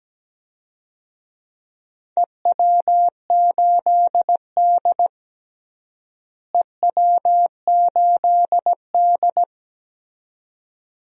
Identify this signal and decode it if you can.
Morse code, heard as a keyed tone, about 17 wpm, 700 Hz. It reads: EW8D EW8D
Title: DXCC-100 Call Signs - 4 Characters - Repeated 2x - 17wpm